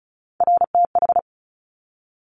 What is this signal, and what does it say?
Morse code, heard as a keyed tone, about 36 wpm, 700 Hz. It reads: RTH
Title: Top Three Letter Combinations in Words 35wpm